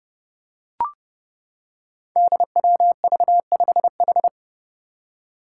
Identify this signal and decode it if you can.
Morse code, heard as a keyed tone, about 32 wpm, 700 Hz. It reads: DWV5H
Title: Five Challenging Letter-Number Combinations 30wpm